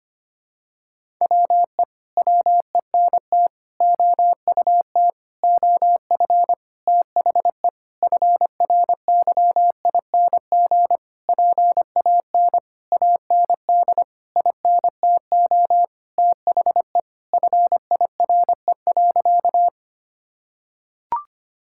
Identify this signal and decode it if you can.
Morse code, heard as a keyed tone, about 25 wpm, 700 Hz. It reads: WE WENT OUT OF THE FRYING PAN AND INTO THE FIRE.
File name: practice / 25wpm / Sentences with 9-Word Phrases - 25wpm